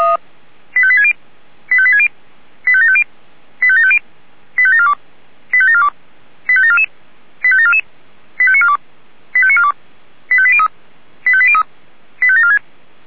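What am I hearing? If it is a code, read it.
Five-tone selective call (ZVEI), heard as quick runs of five tones, tones of about 70 ms. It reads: 85860 85860 85750 85750 85742 85742 85744 85744 85942 85942 85803 85803 85746
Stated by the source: ZVEI five-tone selective call 85724 received on SDR.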